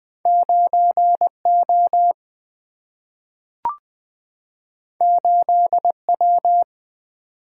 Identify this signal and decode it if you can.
Morse code, heard as a keyed tone, about 20 wpm, 700 Hz. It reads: 9 O 8 W
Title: Single-Letter Single-Number 20wpm